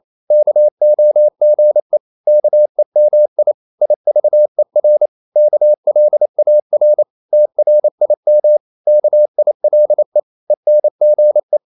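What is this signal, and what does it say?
Morse code, heard as a keyed tone, about 28 wpm, 600 Hz. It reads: KOGE KEMI IVER KLAR TRIM KILE ENGE
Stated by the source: Danish short words sent in Morse at 28 wpm, from CW med Gnister.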